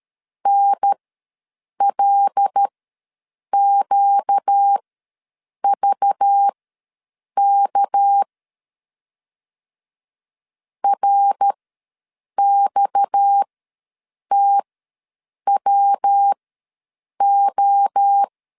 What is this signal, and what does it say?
Morse code, heard as a keyed tone, about 13 wpm, spaced as for 7 wpm, 800 Hz. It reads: NLQVK RXTWO
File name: LEZIONI / 6 PKX0 / 6 PKX ws 12wpm 3106